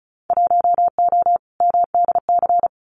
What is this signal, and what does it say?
Morse code, heard as a keyed tone, about 36 wpm, 700 Hz. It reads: 1O MDC
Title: Winter Field Day Exchanges 35wpm